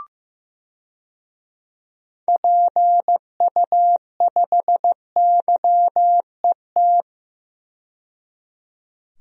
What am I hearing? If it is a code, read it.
Morse code, heard as a keyed tone, about 15 wpm, 700 Hz. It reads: PU5YET
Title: DXCC-100 Call Signs - 6 Characters - 15wpm